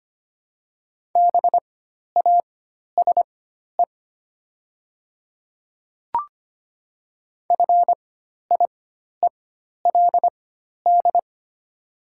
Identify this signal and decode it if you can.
Morse code, heard as a keyed tone, about 25 wpm, spaced as for 12 wpm, 700 Hz. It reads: BASE FIELD
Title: 4x Character Spacing - Top 500 Words - 25wpm